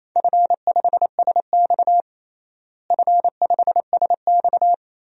Morse code, heard as a keyed tone, about 28 wpm, 700 Hz. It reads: F5SX F5SX
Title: DXCC-100 Call Signs - 4 Characters - Repeated 3x - 28wpm